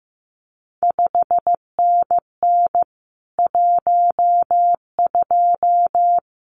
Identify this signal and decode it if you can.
Morse code, heard as a keyed tone, about 15 wpm, 700 Hz. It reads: 5NN 12
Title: CQ WW DX Contest Exchange Encore - 15wpm